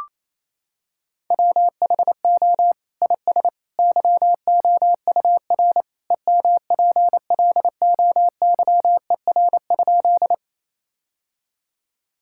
Morse code, heard as a keyed tone, about 28 wpm, 700 Hz. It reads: WHO IS YOUR EMPLOYER?
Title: Sentences from Top 1600 Words 28wpm